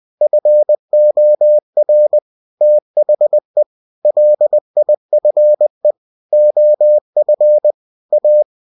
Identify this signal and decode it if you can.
Morse code, heard as a keyed tone, about 20 wpm, 600 Hz. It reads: FOR THE LIFE OF A